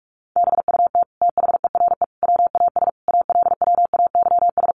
Morse code, heard as a keyed tone, about 44 wpm, 700 Hz. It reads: BUT THERE WAS ALWAYS
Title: Sentences from Top 900 Words 45wpm